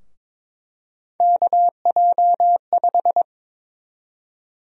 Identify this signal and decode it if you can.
Morse code, heard as a keyed tone, about 21 wpm, 700 Hz. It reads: KJ5
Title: USA Call Sign Prefixes - 22wpm